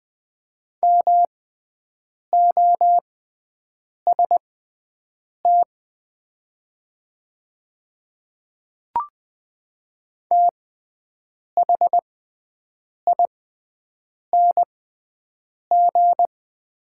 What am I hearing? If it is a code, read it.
Morse code, heard as a keyed tone, about 20 wpm, spaced as for 7 wpm, 690 Hz. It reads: MOST THING
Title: 6x Character Spacing - Top 100 Words - 20wpm